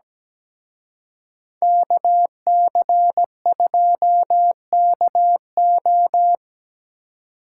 Morse code, heard as a keyed tone, about 20 wpm, 700 Hz. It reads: KC2KO
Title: Speed Racing - USA 2x2 Call Signs 17wpm